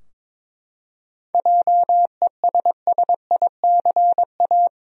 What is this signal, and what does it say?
Morse code, heard as a keyed tone, about 22 wpm, 700 Hz. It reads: JESSICA
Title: USA Common Names - 22wpm